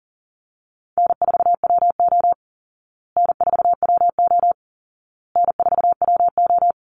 Morse code, heard as a keyed tone, about 40 wpm, 700 Hz. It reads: N4WO N4WO N4WO
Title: Easy Call Signs Repeated 3x - 40wpm